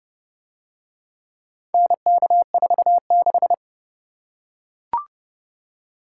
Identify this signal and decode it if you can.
Morse code, heard as a keyed tone, about 28 wpm, 700 Hz. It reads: NK46